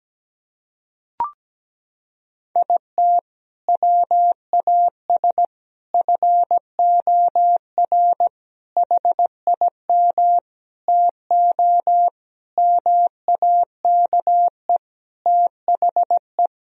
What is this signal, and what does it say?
Morse code, heard as a keyed tone, about 17 wpm, 700 Hz. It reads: IT WAS FOR HIM TO MAKE THE
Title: Sentences from Top 1000 Words - Review Part 1 - 17wpm